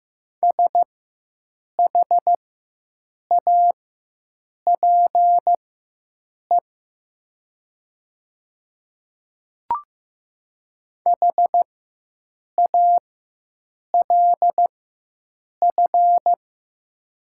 Morse code, heard as a keyed tone, about 15 wpm, spaced as for 7 wpm, 700 Hz. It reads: SHAPE HALF
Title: 4x Character Spacing - Top 500 Words - 15wpm